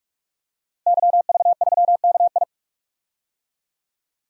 Morse code, heard as a keyed tone, about 46 wpm, 700 Hz. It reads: YV3KI
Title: DXCC-100 Call Signs - 5 Characters - Repeated 2x - 45wpm